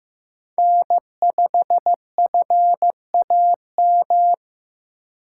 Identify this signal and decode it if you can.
Morse code, heard as a keyed tone, about 18 wpm, 700 Hz. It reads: N5FAM
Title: Speed Racing - USA 1x3 Call Signs 15wpm